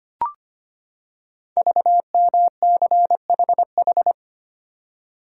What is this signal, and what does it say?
Morse code, heard as a keyed tone, about 25 wpm, 700 Hz. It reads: VMCHH
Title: Code Groups - 5 Letter:Numbers - 25wpm